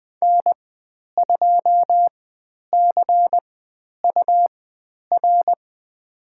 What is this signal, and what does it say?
Morse code, heard as a keyed tone, about 20 wpm, spaced as for 10 wpm, 700 Hz. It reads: N2CUR
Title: 109 Lesson 18g - Letter m - Call Signs 20wpm